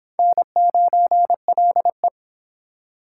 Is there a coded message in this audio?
Yes: Morse code from 0.2 to 2.1 s, heard as a keyed tone, about 26 wpm, 700 Hz. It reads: N9LE